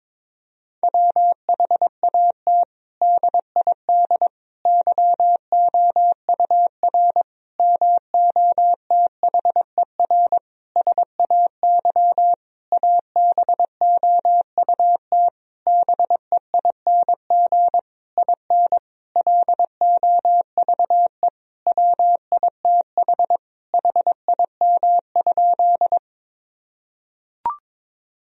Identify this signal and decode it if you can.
Morse code, heard as a keyed tone, about 21 wpm, 700 Hz. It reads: WHAT DID YOUR MOTHER SAY ABOUT BEING IN LOVE WITH HIM?
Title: Sentences from Top 1000 Words - Review Part 3 - 22wpm